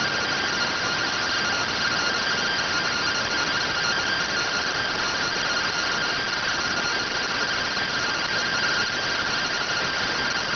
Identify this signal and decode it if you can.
Two-tone teleprinter signal, about 170 Hz shift.